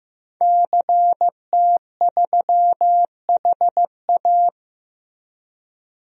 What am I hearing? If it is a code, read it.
Morse code, heard as a keyed tone, about 15 wpm, 700 Hz. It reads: CT3HA
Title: DXCC-100 Call Signs - 5 Characters - Repeated 2x - 15wpm